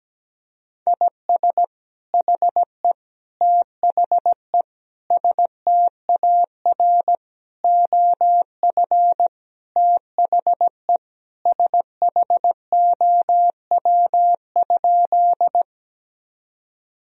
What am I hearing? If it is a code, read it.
Morse code, heard as a keyed tone, about 17 wpm, 700 Hz. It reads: IS HE THE STAR OF THE SHOW?
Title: Sentences from Top 1000 Words - Review Part 2 - 17wpm